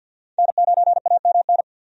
Morse code, heard as a keyed tone, about 52 wpm, 700 Hz. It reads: N9AMN
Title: USA Call Signs - 5 Characters - 50wpm